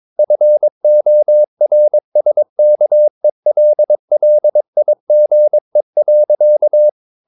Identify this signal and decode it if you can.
Morse code, heard as a keyed tone, about 22 wpm, 600 Hz. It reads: FORSKELLIGE.
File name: Ferie QRS 22wpm